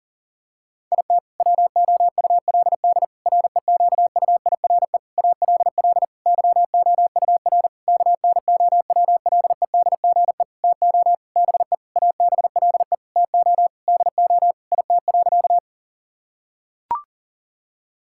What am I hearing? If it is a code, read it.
Morse code, heard as a keyed tone, about 40 wpm, 700 Hz. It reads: IT WOULD REQUIRE ALL YOUR KNOWLEDGE TO BE ABLE TO DO IT.
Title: Sentences from Top 1000 Words - Review Part 6 - 40wpm